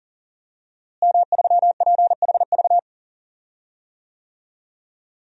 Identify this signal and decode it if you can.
Morse code, heard as a keyed tone, about 40 wpm, 700 Hz. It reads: M3PHV
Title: DXCC-100 Call Signs - 5 Characters - Repeated 2x - 40wpm